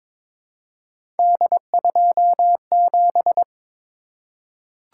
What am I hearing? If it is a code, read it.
Morse code, heard as a keyed tone, about 22 wpm, 700 Hz. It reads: D27
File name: practice / 22wpm / Three Letter-Number Combinations - 22wpm